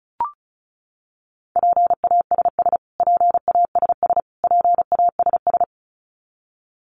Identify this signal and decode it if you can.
Morse code, heard as a keyed tone, about 34 wpm, 700 Hz. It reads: PASS PASS PASS
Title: Top 500 Words Repeated 35wpm